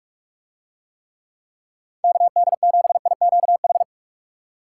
Morse code, heard as a keyed tone, about 46 wpm, 700 Hz. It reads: KD7IQH